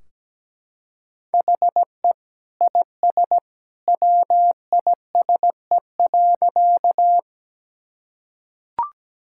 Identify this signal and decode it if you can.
Morse code, heard as a keyed tone, about 17 wpm, 700 Hz. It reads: HE IS WISE.
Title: Sentences from Top 1600 Words 17wpm